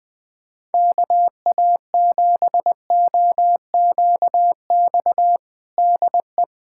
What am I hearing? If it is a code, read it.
Morse code, heard as a keyed tone, about 20 wpm, 700 Hz. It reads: KA7OQX DE
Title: USA Callsign DE Callsign - 6 Characters - 20wpm